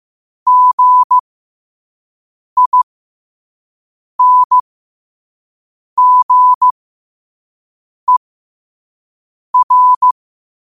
Morse code, heard as a keyed tone, about 15 wpm, spaced as for 5 wpm, 1000 Hz. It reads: GINGER
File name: qotdfiles / 5wpm